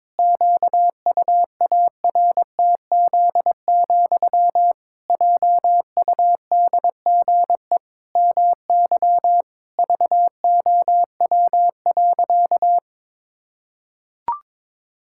Morse code, heard as a keyed tone, about 22 wpm, 700 Hz. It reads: QUARTZ, JUDGE MY VOW.
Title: Pangram Challenge v2 - 22wpm